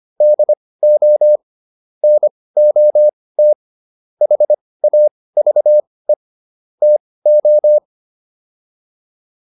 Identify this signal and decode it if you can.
Morse code, heard as a keyed tone, about 25 wpm, spaced as for 18 wpm, 600 Hz. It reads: DO NOT HAVE TO